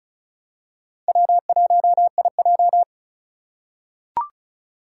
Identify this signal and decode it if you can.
Morse code, heard as a keyed tone, about 36 wpm, 700 Hz. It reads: W1IJ
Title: USA Call Signs - 4 Characters - 2x Character Spacing - 35wpm